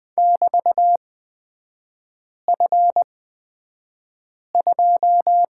Morse code, heard as a keyed tone, about 20 wpm, spaced as for 5 wpm, 700 Hz. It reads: <BT>F2
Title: 224 Lesson 36e - BT - Three Characters 20wpm